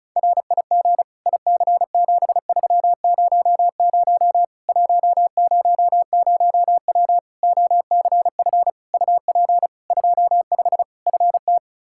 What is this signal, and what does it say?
Morse code, heard as a keyed tone, about 36 wpm, 700 Hz. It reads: RIG IC7300 100W OCF UP 25 FT
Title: Practice QSO - Exchange 2 - Info - 35wpm